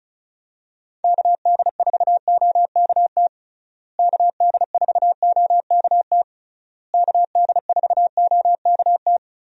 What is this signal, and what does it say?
Morse code, heard as a keyed tone, about 34 wpm, 700 Hz. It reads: KD4OKT KD4OKT KD4OKT
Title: USA Call Signs - 6 Characters - Repeated 3x - 35wpm